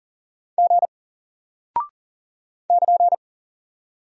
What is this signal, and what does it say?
Morse code, heard as a keyed tone, about 40 wpm, 700 Hz. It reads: G <KN>
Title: Single Character (Letter:Number:Punctuation:Prosign) Alt 40wpm